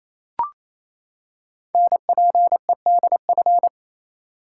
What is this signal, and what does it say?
Morse code, heard as a keyed tone, about 27 wpm, 700 Hz. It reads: NPEDF
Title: Code Groups - 5 Letter:Numbers - 28wpm